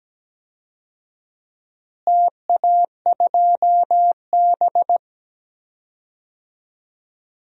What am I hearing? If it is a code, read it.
Morse code, heard as a keyed tone, about 17 wpm, 700 Hz. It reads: TA2B
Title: DXCC-100 Call Signs - 4 Characters - Repeated 2x - 17wpm